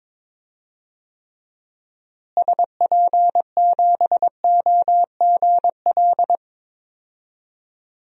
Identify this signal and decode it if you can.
Morse code, heard as a keyed tone, about 22 wpm, 700 Hz. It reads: SP7OGL
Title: DXCC-100 Call Signs - 6 Characters - Repeated 2x - 22wpm